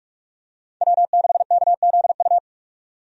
Morse code, heard as a keyed tone, about 46 wpm, 700 Hz. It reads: W6KZU
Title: USA Call Signs - 5 Characters - 45wpm